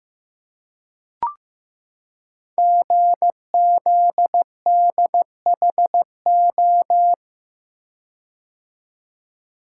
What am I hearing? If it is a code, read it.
Morse code, heard as a keyed tone, about 15 wpm, 700 Hz. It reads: GZDHO
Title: Code Groups - 5 Letters - 15wpm